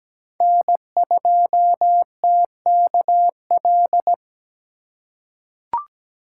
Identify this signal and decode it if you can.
Morse code, heard as a keyed tone, about 17 wpm, 700 Hz. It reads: N2TKL
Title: USA Call Signs - 5 Characters - 3x Character Spacing - 17wpm